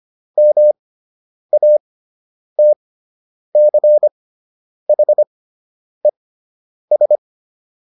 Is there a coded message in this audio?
Yes: Morse code from 0.4 to 7.2 s, heard as a keyed tone, about 25 wpm, spaced as for 9 wpm, 600 Hz. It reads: MATCHES